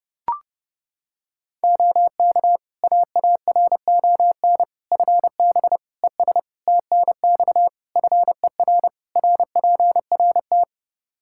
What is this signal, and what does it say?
Morse code, heard as a keyed tone, about 30 wpm, 700 Hz. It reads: OK AARON FB ES TNX FER RPRT
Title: K1USN SST QSO Practice 30wpm